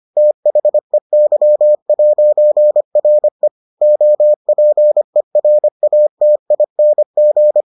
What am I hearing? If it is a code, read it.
Morse code, heard as a keyed tone, about 25 wpm, 600 Hz. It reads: THEY'RE OPERATING